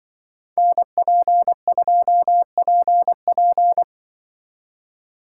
Morse code, heard as a keyed tone, about 23 wpm, 700 Hz. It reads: NP2PP
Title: Speed Racing - USA 2x2 Call Signs 20wpm